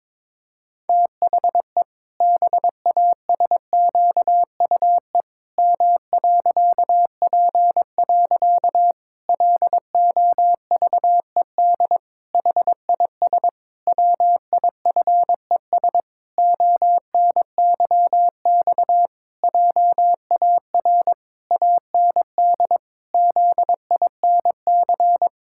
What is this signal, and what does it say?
Morse code, heard as a keyed tone, about 22 wpm, 700 Hz. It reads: THE BASQUE M.P. LOVED HIS WIFES ONYX JAR AND ZINC
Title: Pangram Challenge v2 - 22wpm